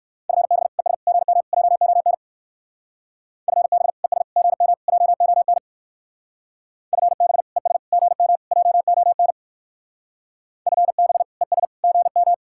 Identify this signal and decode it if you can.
Morse code, heard as a keyed tone, about 58 wpm, 700 Hz. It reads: FB ES GM JON FB ES GM JON FB ES GM JON FB ES GM